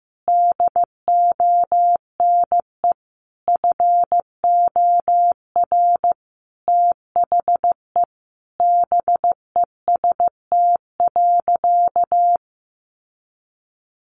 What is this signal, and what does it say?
Morse code, heard as a keyed tone, about 15 wpm, 700 Hz. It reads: DONE FOR THE BEST.
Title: Sentences from Top 600 Words 15wpm